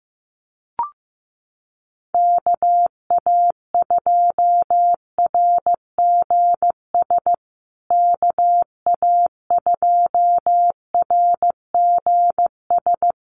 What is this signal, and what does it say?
Morse code, heard as a keyed tone, about 15 wpm, 700 Hz. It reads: KA2RGS KA2RGS
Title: USA Call Signs Repeated 3x 15wpm